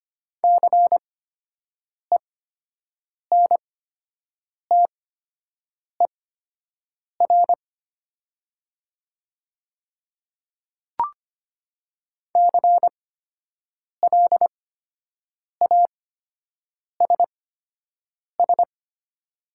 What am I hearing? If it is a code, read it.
Morse code, heard as a keyed tone, about 25 wpm, spaced as for 7 wpm, 700 Hz. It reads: CENTER CLASS
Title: 8x Character Spacing - Top 500 Words - 25wpm